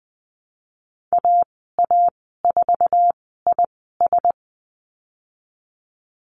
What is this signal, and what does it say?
Morse code, heard as a keyed tone, about 20 wpm, spaced as for 15 wpm, 700 Hz. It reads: AA4IS
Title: 055 Lesson 10g - Letter r - Call Signs 20wpm